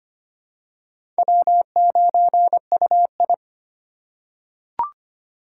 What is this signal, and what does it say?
Morse code, heard as a keyed tone, about 25 wpm, 700 Hz. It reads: W9UI
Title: USA Call Signs - 8x Character Spacing - 25wpm